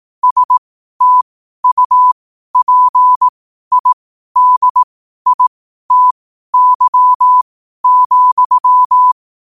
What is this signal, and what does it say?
Morse code, heard as a keyed tone, about 18 wpm, spaced as for 12 wpm, 1000 Hz. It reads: STUPIDITY,
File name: qotdfiles / 12wpm